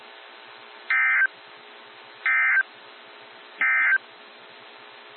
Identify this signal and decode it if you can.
Emergency Alert System (SAME) tones, heard as harsh buzzing bursts. It is heard from 0.8 to 4.1 s.
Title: NOAA Weather Radio - WXJ73